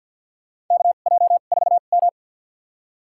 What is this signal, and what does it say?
Morse code, heard as a keyed tone, about 53 wpm, spaced as for 37 wpm, 700 Hz. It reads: KJ4M
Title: USA Call Signs - 2x Character Spacing - 50wpm